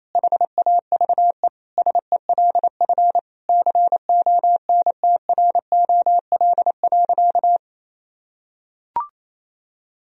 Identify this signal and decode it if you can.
Morse code, heard as a keyed tone, about 28 wpm, 700 Hz. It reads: HAVE SELF CONTROL.
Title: Sentences from Top 1000 Words - Review Part 3 - 28wpm